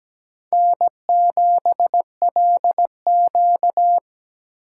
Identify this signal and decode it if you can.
Morse code, heard as a keyed tone, about 17 wpm, 700 Hz. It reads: N7LQ
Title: USA Call Signs - 4 Characters - 2x Character Spacing - 17wpm